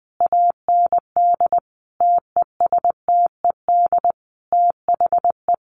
Morse code, heard as a keyed tone, about 19 wpm, 700 Hz. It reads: AND TESTED THE
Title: A Sound Of Thunder 20wpm